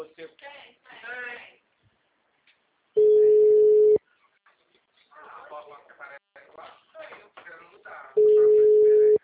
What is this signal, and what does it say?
Ringback tone (Europe, 425 Hz), with speech.
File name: out- -2014- - - .60070